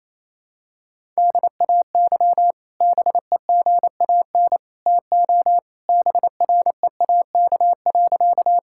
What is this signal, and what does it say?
Morse code, heard as a keyed tone, about 27 wpm, 700 Hz. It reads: DAY BEGAN TO BREAK.
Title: Sentences from Top 1000 Words - Review Part 6 - 28wpm